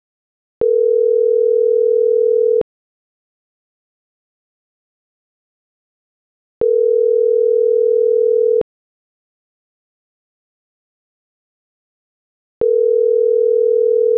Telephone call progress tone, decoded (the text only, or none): ringback tone